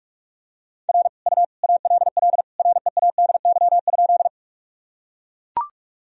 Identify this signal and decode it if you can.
Morse code, heard as a keyed tone, about 41 wpm, 700 Hz. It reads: R U ALL READY?
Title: Sentences from Top 400 Words and Abbreviations 45wpm